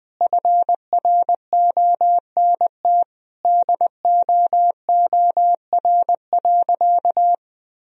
Morse code, heard as a keyed tone, about 19 wpm, 700 Hz. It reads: FRONT DOOR.
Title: Sentences from Top 1000 Words - Review Part 2 - 20wpm